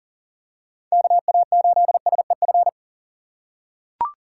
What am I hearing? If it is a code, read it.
Morse code, heard as a keyed tone, about 40 wpm, 700 Hz. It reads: KA8SEF
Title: USA Call Signs - 6 Characters - 4x Character Spacing - 40wpm